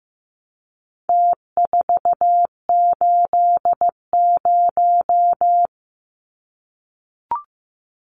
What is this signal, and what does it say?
Morse code, heard as a keyed tone, about 15 wpm, 700 Hz. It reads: T480
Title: Numbers - Contest Style - 4-Digit 15wpm